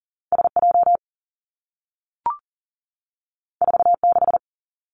Morse code, heard as a keyed tone, about 40 wpm, 700 Hz. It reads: S J 4 6
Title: Two Challenging Letter-Number Combinations 40wpm